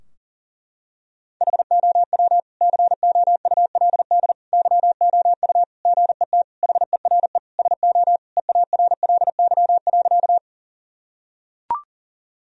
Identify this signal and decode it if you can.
Morse code, heard as a keyed tone, about 40 wpm, 700 Hz. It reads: HOW COULD YOU GET HERE SO EARLY.
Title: Sentences from Top 1000 Words - Review Part 2 - 40wpm